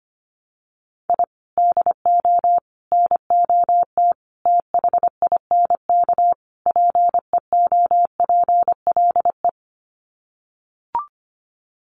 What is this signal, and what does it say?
Morse code, heard as a keyed tone, about 25 wpm, 700 Hz. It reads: I DO NOT THINK PEOPLE
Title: Sets of 5 Words Encore2 25wpm